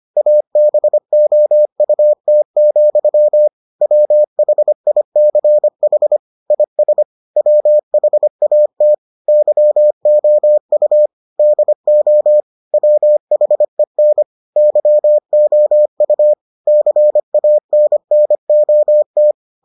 Morse code, heard as a keyed tone, about 25 wpm, 600 Hz. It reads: ABOUT, WHICH IS WHAT YOU DO WHEN YOU CANNOT